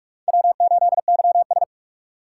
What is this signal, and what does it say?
Morse code, heard as a keyed tone, about 46 wpm, 700 Hz. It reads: W8YS
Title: USA Call Signs - 4x Character Spacing - 45wpm